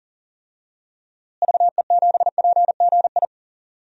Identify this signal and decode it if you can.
Morse code, heard as a keyed tone, about 40 wpm, 700 Hz. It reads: VE7PGI
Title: DXCC-100 Call Signs - 6 Characters - Repeated 2x - 40wpm